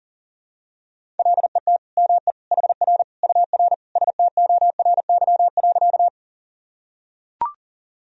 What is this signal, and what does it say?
Morse code, heard as a keyed tone, about 40 wpm, 700 Hz. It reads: LET ME HR UR STORY.
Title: Sentences from Top 300 Words and Abbreviations 40wpm